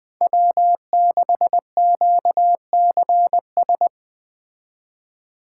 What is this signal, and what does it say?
Morse code, heard as a keyed tone, about 20 wpm, 700 Hz. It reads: W6QCS